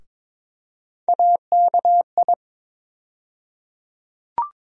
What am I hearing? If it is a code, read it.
Morse code, heard as a keyed tone, about 22 wpm, 700 Hz. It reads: AKI
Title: USA Call Sign Postfixes - 3 Letters 22wpm